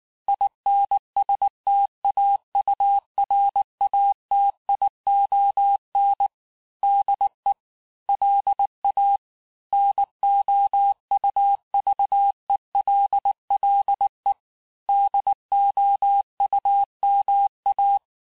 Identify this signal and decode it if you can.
Morse code, heard as a keyed tone, about 19 wpm, 800 Hz. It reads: INSTAURATION DE LA NOUVELLE DOUMA
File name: cours-du-lundi files / cours-1595-teste-CLAIR-vitesse18-MPM-puis23MPM